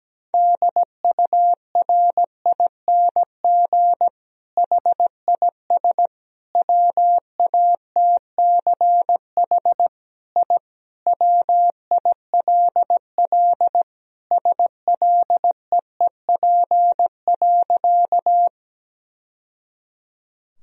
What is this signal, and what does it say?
Morse code, heard as a keyed tone, about 17 wpm, 700 Hz. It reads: DURING HIS WATCH I WILL SLEEP.
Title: Sentences from Top 1000 Words - Review Part 2 - 17wpm